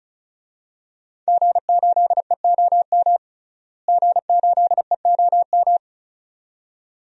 Morse code, heard as a keyed tone, about 34 wpm, 700 Hz. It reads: G8EOM G8EOM
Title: DXCC-100 Call Signs - 5 Characters - Repeated 2x - 35wpm